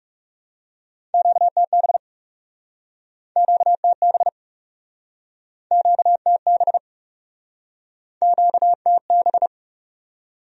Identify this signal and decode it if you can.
Morse code, heard as a keyed tone, about 41 wpm, 700 Hz. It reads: QTB QTB QTB QTB